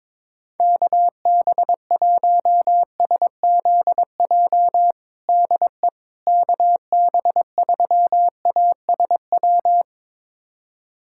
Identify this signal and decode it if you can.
Morse code, heard as a keyed tone, about 22 wpm, 700 Hz. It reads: KB1SZJ DE KB3ASW
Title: USA Callsign DE Callsign - 6 Characters - 22wpm